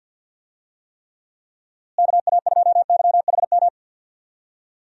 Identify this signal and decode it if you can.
Morse code, heard as a keyed tone, about 52 wpm, 700 Hz. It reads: KA2YHM
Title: USA Call Signs - 6 Characters - Repeated 3x - 50wpm